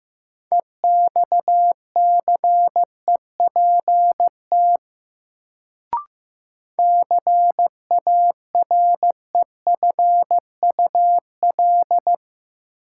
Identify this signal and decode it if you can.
Morse code, heard as a keyed tone, about 15 wpm, 700 Hz. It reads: EXCEPT CAREFUL
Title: Sentences from Top 1600 Words - New Word Drill Alt - 15wpm